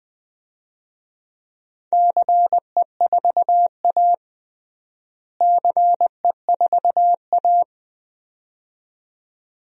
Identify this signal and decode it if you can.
Morse code, heard as a keyed tone, about 21 wpm, 700 Hz. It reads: CE4A CE4A
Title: DXCC-100 Call Signs - 4 Characters - Repeated 2x - 20wpm